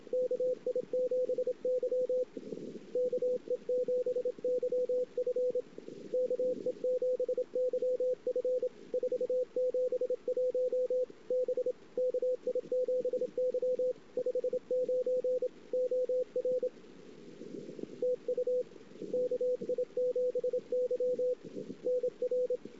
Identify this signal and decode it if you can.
Morse code, heard as a keyed tone, about 26 wpm, 500 Hz, with radio noise. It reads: KI7Y KE7YF KE7YF 471 B KI7Y 59 OR TU KI7Y NR
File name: ss-cw